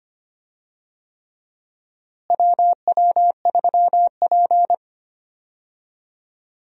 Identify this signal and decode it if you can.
Morse code, heard as a keyed tone, about 25 wpm, 700 Hz. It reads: WW3P
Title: USA Call Signs - 4 Characters - Repeated 3x - 25wpm